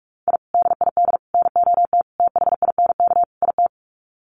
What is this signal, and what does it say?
Morse code, heard as a keyed tone, about 44 wpm, 700 Hz. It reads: I DID NOT THINK IT
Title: Sets of 5 Words Encore2 45wpm